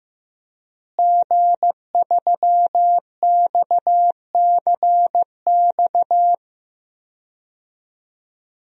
Morse code, heard as a keyed tone, about 15 wpm, 700 Hz. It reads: G3XCX
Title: DXCC-100 Call Signs - 5 Characters - Repeated 3x - 15wpm